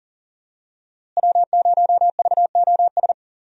Morse code, heard as a keyed tone, about 38 wpm, 700 Hz. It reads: W0VOS
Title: USA Call Signs - 5 Characters - Repeated 3x - 40wpm